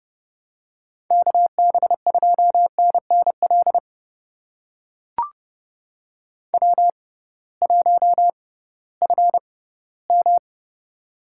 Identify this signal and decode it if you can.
Morse code, heard as a keyed tone, about 31 wpm, spaced as for 18 wpm, 700 Hz. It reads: KB2NNL W1FM
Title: USA Call Signs - 6x Character Spacing - 30wpm